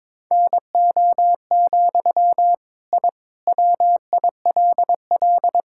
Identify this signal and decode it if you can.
Morse code, heard as a keyed tone, about 22 wpm, 700 Hz. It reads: NO, I WILL
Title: Sentences from Top 1000 Words - Review Part 5 - 22wpm